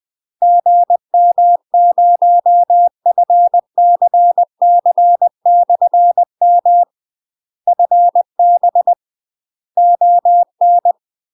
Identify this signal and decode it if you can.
Morse code, heard as a keyed tone, about 20 wpm, 700 Hz. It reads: GM0FCC/M FB ON
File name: UK MASTER QSOs TRAINING / 20WPM / UK MASTER QSOs 0050